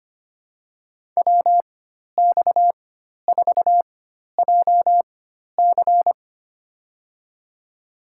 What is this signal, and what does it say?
Morse code, heard as a keyed tone, about 25 wpm, spaced as for 12 wpm, 700 Hz. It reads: WX4JC
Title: USA Call Signs - 5 Characters - 4x Character Spacing - 25wpm